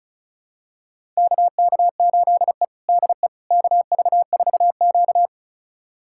Morse code, heard as a keyed tone, about 36 wpm, 700 Hz. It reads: KK8E DE KV4Q
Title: USA Callsign DE Callsign - 4 Characters - 35wpm